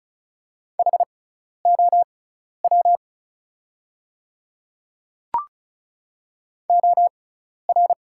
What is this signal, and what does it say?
Morse code, heard as a keyed tone, about 36 wpm, spaced as for 12 wpm, 700 Hz. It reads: HOW OR
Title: 6x Character Spacing - Top 200 Words - 35wpm